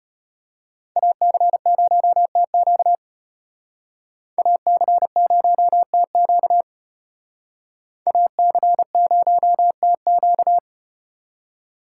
Morse code, heard as a keyed tone, about 32 wpm, 700 Hz. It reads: AC0TQ AC0TQ AC0TQ